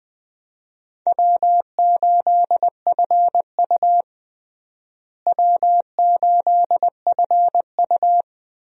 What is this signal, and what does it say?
Morse code, heard as a keyed tone, about 20 wpm, 700 Hz. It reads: W8FU W8FU
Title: USA Call Signs - 4 Characters - Repeated 2x - 20wpm